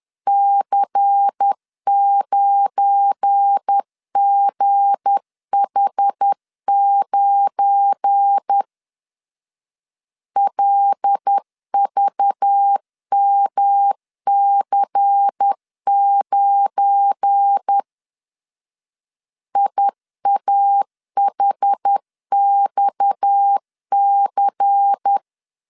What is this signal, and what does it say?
Morse code, heard as a keyed tone, about 10 wpm, 800 Hz. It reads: C9GH9 LVMC9 IAHXC
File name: LEZIONI / 7 GF9Z / 7 GF9 10wpm 2747